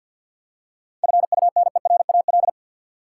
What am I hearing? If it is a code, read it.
Morse code, heard as a keyed tone, about 47 wpm, 700 Hz. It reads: FUNERAL
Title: Seven Letter Words - 50wpm